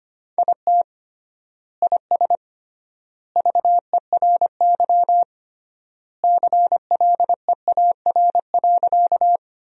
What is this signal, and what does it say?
Morse code, heard as a keyed tone, about 25 wpm, 700 Hz. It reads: IT IS VERY CLEAR.
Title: 3x Word-Spacing - Sentences from Top 500 Words - 25wpm